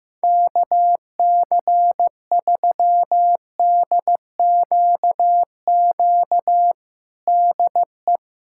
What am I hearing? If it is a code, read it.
Morse code, heard as a keyed tone, about 16 wpm, 700 Hz. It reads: KC3DQQ DE
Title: USA Callsign DE Callsign 15wpm